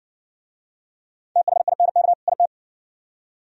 Morse code, heard as a keyed tone, about 57 wpm, 700 Hz. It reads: THINK IT